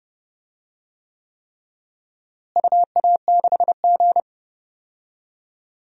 Morse code, heard as a keyed tone, about 30 wpm, 700 Hz. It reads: UA6G
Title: DXCC-100 Call Signs - 4 Characters - Repeated 3x - 30wpm